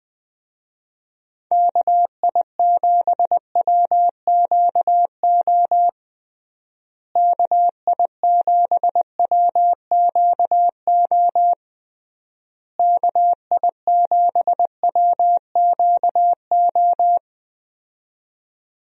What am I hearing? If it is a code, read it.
Morse code, heard as a keyed tone, about 21 wpm, 700 Hz. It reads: KI7WQO KI7WQO KI7WQO
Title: USA Call Signs - 6 Characters - Repeated 3x - 20wpm